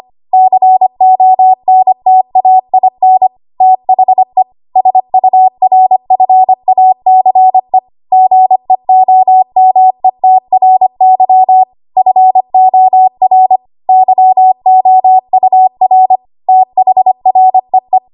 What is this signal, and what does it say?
Morse code, heard as a keyed tone, about 25 wpm, 750 Hz. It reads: CONTAIN THE SURFACE GEOMETRY FOR YOUR THREE
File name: Archive / 25 WPM / 25WPM